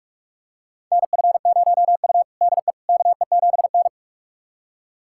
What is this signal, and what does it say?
Morse code, heard as a keyed tone, about 46 wpm, 700 Hz. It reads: NF0U DE KE7N
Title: USA Callsign DE Callsign - 4 Characters - 45wpm